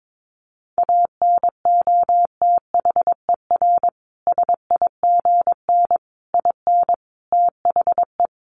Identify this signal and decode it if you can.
Morse code, heard as a keyed tone, about 22 wpm, 700 Hz. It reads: ANOTHER SIGN IN THE
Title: Sentences from Top 1000 Words 22wpm